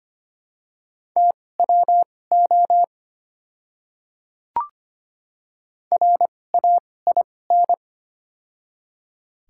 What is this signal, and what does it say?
Morse code, heard as a keyed tone, about 25 wpm, spaced as for 18 wpm, 700 Hz. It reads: TWO RAIN
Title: 2x Character Spacing - Top 300 Words - 25wpm